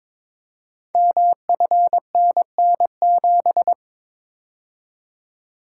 Morse code, heard as a keyed tone, about 23 wpm, 700 Hz. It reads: MFNN7
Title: Code Groups - 5 Letter:Numbers - 22wpm